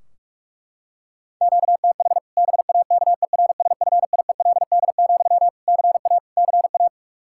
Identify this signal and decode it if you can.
Morse code, heard as a keyed tone, about 46 wpm, 700 Hz. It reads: QTH BAKERSFIELD, CA CA
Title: Practice QSO - Exchange 1 - Beginning - 45wpm